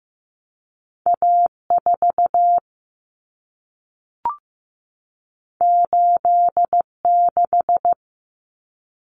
Morse code, heard as a keyed tone, about 15 wpm, 700 Hz. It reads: A4 86
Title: Common Cut Numbers - 2-Digit 15wpm